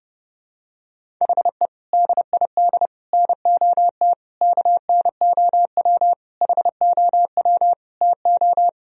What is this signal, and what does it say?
Morse code, heard as a keyed tone, about 32 wpm, 700 Hz. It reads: HE DID NOT KNOW HOW TO
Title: Sentences from Top 1000 Words - Review Part 1 - 30wpm